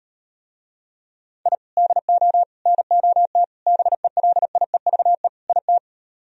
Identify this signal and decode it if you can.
Morse code, heard as a keyed tone, about 38 wpm, 700 Hz. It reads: I DO NOT BELIEVE IT